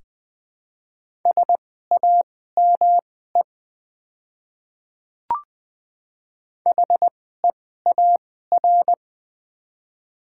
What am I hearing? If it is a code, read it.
Morse code, heard as a keyed tone, about 20 wpm, spaced as for 15 wpm, 700 Hz. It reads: SAME HEAR
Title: 2x Character Spacing - Top 300 Words - 20wpm